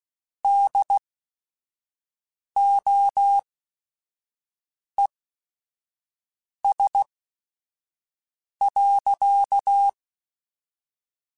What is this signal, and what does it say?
Morse code, heard as a keyed tone, about 16 wpm, spaced as for 5 wpm, 800 Hz. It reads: DOES.
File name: Technical Subjects / Technical Subjects-mp3s / Sw Pwr Sup @5